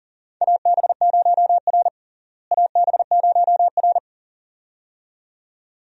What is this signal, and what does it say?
Morse code, heard as a keyed tone, about 40 wpm, 700 Hz. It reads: AB0R AB0R
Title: USA Call Signs - 4 Characters - Repeated 2x - 40wpm